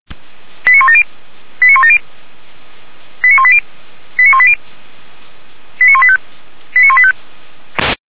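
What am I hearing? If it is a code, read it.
Five-tone selective call (ZVEI), heard as quick runs of five tones, tones of about 70 ms. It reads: 89170 89170 89180 89180 89185 89185
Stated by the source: ZVEI five-tone selective call received by SDR, Feuerwehr Orlamünde fire brigade.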